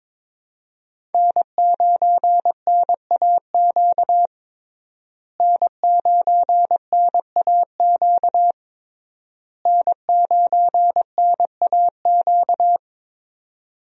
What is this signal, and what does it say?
Morse code, heard as a keyed tone, about 22 wpm, 700 Hz. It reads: N9NAQ N9NAQ N9NAQ
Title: USA Call Signs - 5 Characters - Repeated 3x - 22wpm